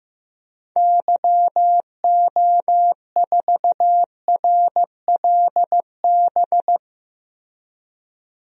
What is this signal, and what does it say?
Morse code, heard as a keyed tone, about 15 wpm, 700 Hz. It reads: YO4RLB
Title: DXCC-100 Call Signs - 6 Characters - Repeated 2x - 15wpm